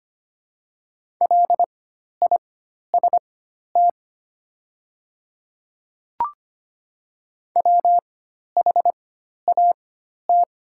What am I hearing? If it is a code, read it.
Morse code, heard as a keyed tone, about 25 wpm, spaced as for 12 wpm, 700 Hz. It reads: LIST WHAT
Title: 4x Character Spacing - Top 500 Words - 25wpm